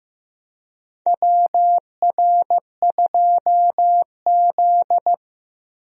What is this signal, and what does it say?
Morse code, heard as a keyed tone, about 15 wpm, 700 Hz. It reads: WR2Z
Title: USA Call Signs - 4 Characters - 15wpm